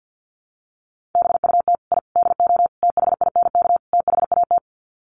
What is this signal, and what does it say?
Morse code, heard as a keyed tone, about 50 wpm, 700 Hz. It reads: BUT I DO THINK THAT
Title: Sets of 5 Words Encore2 50wpm